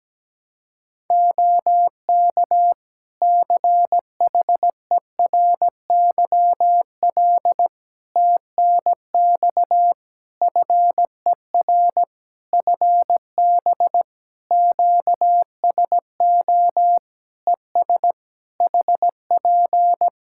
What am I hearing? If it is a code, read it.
Morse code, heard as a keyed tone, about 17 wpm, 700 Hz. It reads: OK CHERYL TNX FER FB QSO ES HP
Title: Practice QSO - Exchange 4 - Ending - 17wpm